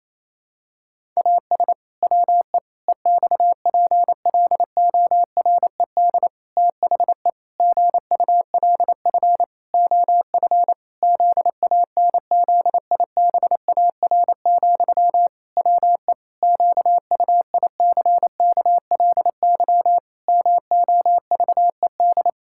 Morse code, heard as a keyed tone, about 28 wpm, 700 Hz. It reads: AS WE EXPLORED THE GULF OF ZANZIBAR, WE QUICKLY MOVED
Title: Pangram Challenge v2 - 28wpm